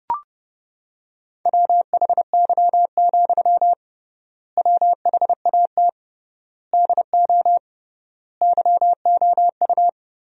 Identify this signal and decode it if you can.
Morse code, heard as a keyed tone, about 30 wpm, 700 Hz. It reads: WHY, WHAT DO YOU
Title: 3x Word-Spacing - Sentences from Top 200 Words - 30wpm